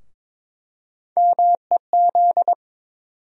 Morse code, heard as a keyed tone, about 22 wpm, 700 Hz. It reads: MEZ
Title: USA Call Sign Postfixes - 3 Letters 22wpm